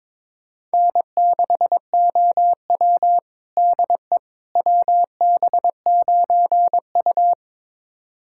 Morse code, heard as a keyed tone, about 22 wpm, 700 Hz. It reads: N6OW DE WB9U
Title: USA Callsign DE Callsign - 4 Characters - 22wpm